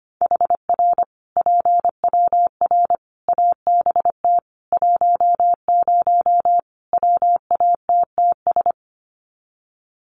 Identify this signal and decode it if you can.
Morse code, heard as a keyed tone, about 25 wpm, 700 Hz. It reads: HR PWR ABT 10 WATTS
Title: Common QSO Phrases 25wpm